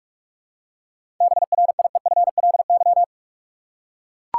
Morse code, heard as a keyed tone, about 43 wpm, 700 Hz. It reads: BRIEFLY
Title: Seven Letter Words - 45wpm